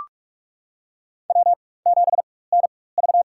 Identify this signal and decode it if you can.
Morse code, heard as a keyed tone, about 46 wpm, spaced as for 21 wpm, 700 Hz. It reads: W7NV
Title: USA Call Signs - 4 Characters - 4x Character Spacing - 45wpm